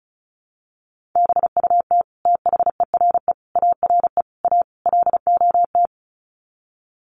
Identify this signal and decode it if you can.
Morse code, heard as a keyed tone, about 34 wpm, 700 Hz. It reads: BUT THERE ARE A LOT
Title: Sets of 5 Words Encore2 35wpm